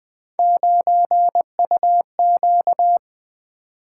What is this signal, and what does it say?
Morse code, heard as a keyed tone, about 20 wpm, 700 Hz. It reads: 9UQ
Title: Three Letter-Number Combinations - 20wpm